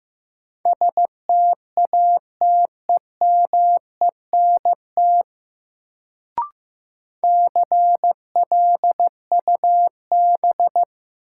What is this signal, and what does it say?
Morse code, heard as a keyed tone, about 15 wpm, 700 Hz. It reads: STATEMENT CLUB
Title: Sentences from Top 1200 Words - New Word Drill Alt - 15wpm